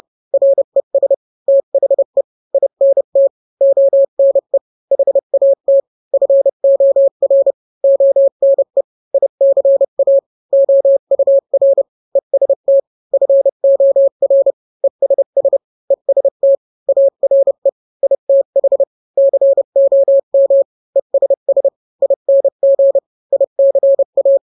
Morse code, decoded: RES THE INT ONE HAT FOR ONE ICA OUR EST FOR ESS EST ARE ITH COM ESS ING ICA